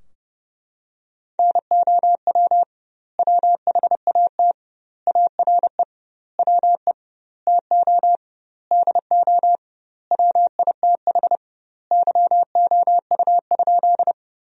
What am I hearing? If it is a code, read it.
Morse code, heard as a keyed tone, about 30 wpm, 700 Hz. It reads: NOW WHAT ARE WE TO DO WITH YOU?
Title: 2x Word-Spacing - Sentences from Top 100 Words - 30wpm